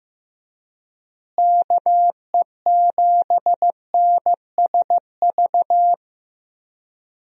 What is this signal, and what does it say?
Morse code, heard as a keyed tone, about 15 wpm, 700 Hz. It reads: KE7NSV
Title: USA Call Signs - 6 Characters - Repeated 2x - 15wpm